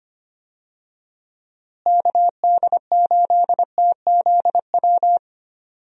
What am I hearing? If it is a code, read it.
Morse code, heard as a keyed tone, about 25 wpm, 700 Hz. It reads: KD8TZW
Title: USA Call Signs - 6 Characters - Repeated 3x - 25wpm